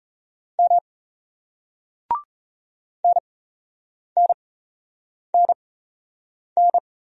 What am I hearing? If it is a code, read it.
Morse code, heard as a keyed tone, about 37 wpm, spaced as for 8 wpm, 700 Hz. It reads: M NNNN